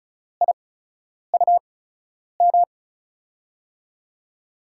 Morse code, heard as a keyed tone, about 34 wpm, spaced as for 10 wpm, 700 Hz. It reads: IUM
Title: USA Call Signs - 8x Character Spacing - 35wpm